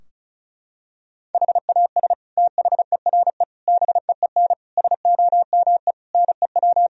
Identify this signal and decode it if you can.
Morse code, heard as a keyed tone, about 36 wpm, 700 Hz. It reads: HAS THERE BEEN SOME NEW
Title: Sentences from Top 1000 Words - Review Part 4 - 35wpm